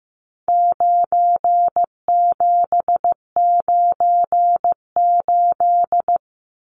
Morse code, heard as a keyed tone, about 14 wpm, 700 Hz. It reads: 9798
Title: Numbers - 4-Digit 15wpm